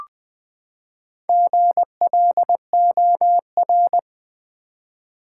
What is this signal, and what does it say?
Morse code, heard as a keyed tone, about 20 wpm, 700 Hz. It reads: GLOR
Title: Build A Word - 4 Letters At A Time - 8-Letter Words - 20wpm